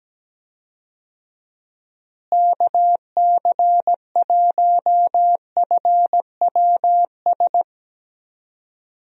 Morse code, heard as a keyed tone, about 17 wpm, 700 Hz. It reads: KC1FWS
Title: USA Call Signs - 6 Characters - Repeated 2x - 17wpm